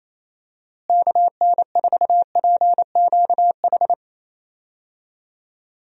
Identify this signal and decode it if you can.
Morse code, heard as a keyed tone, about 28 wpm, 700 Hz. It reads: KN4PQH
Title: USA Call Signs - 6 Characters - Repeated 2x - 28wpm